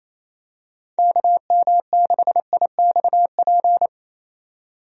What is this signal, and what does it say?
Morse code, heard as a keyed tone, about 28 wpm, 700 Hz. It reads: KM6IXP
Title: USA Call Signs - 6 Characters - 28wpm